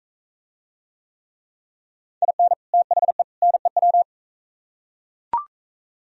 Morse code, heard as a keyed tone, about 43 wpm, 700 Hz. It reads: IN THE NEW